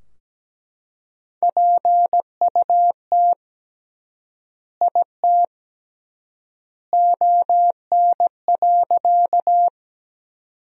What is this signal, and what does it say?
Morse code, heard as a keyed tone, about 17 wpm, 700 Hz. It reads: PUT IT ON.
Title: 3x Word-Spacing - Sentences from Top 200 Words - 17wpm